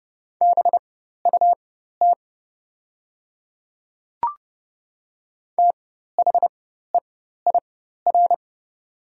Morse code, heard as a keyed tone, about 30 wpm, spaced as for 14 wpm, 690 Hz. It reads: BUT THEIR
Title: 4x Character Spacing - Top 100 Words - 30wpm